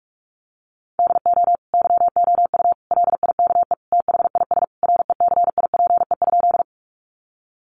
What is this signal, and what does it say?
Morse code, heard as a keyed tone, about 44 wpm, 700 Hz. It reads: DO YOU LIKE THIS RECIPE?
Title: Sentences from Top 1200 Words 45wpm